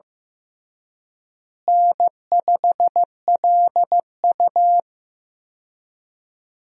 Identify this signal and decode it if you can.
Morse code, heard as a keyed tone, about 15 wpm, 700 Hz. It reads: N5LU
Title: USA Call Signs - 4 Characters - Repeated 3x - 15wpm